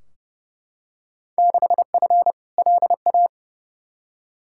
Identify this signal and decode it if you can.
Morse code, heard as a keyed tone, about 31 wpm, 700 Hz. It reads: 6F LA
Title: Field Day Exchanges Encore - 30wpm